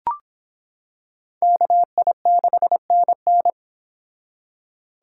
Morse code, heard as a keyed tone, about 26 wpm, 700 Hz. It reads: KI6NN